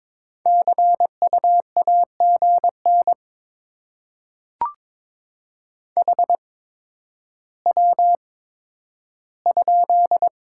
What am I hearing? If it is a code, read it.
Morse code, heard as a keyed tone, about 22 wpm, spaced as for 11 wpm, 700 Hz. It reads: CU AGN HW?
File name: Top 20 QSO Elements - 8x Character Spacing - 22wpm